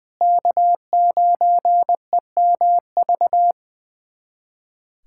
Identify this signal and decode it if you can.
Morse code, heard as a keyed tone, about 20 wpm, 700 Hz. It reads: K9EMV
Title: USA Callsign DE Callsign 20wpm